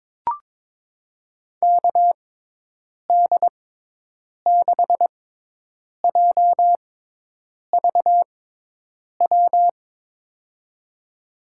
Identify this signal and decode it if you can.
Morse code, heard as a keyed tone, about 22 wpm, spaced as for 8 wpm, 700 Hz. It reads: KD6JVW
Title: USA Call Signs - 6 Characters - 6x Character Spacing - 22wpm